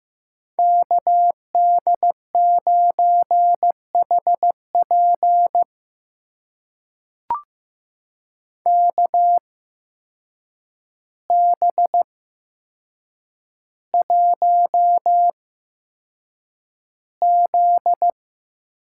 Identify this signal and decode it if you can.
Morse code, heard as a keyed tone, about 15 wpm, spaced as for 7 wpm, 700 Hz. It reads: KD9HP KB1Z
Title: USA Call Signs - 8x Character Spacing - 15wpm